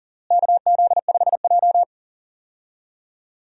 Morse code, heard as a keyed tone, about 40 wpm, 700 Hz. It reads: KZ5J
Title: USA Call Signs - 4 Characters - Repeated 2x - 40wpm